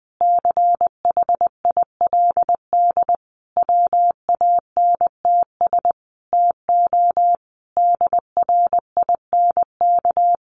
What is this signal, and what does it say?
Morse code, heard as a keyed tone, about 20 wpm, 700 Hz. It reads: CHILD WANTS TO DRINK
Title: Additional Easy Phrases 20wpm